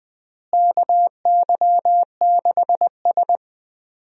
Morse code, heard as a keyed tone, about 20 wpm, 700 Hz. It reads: KY6S
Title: USA Call Signs - 4 Characters - Repeated 3x - 20wpm